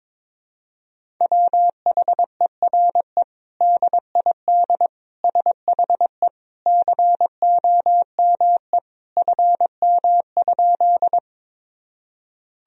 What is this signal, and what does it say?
Morse code, heard as a keyed tone, about 21 wpm, 700 Hz. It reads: WHERE DID SHE COME FM?
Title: Sentences from Top 200 Words and Abbreviations 22wpm